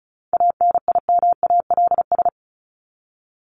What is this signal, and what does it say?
Morse code, heard as a keyed tone, about 34 wpm, 700 Hz. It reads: ANIMALS
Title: ES:S Suffix - Top 100 Words with Base Words 35wpm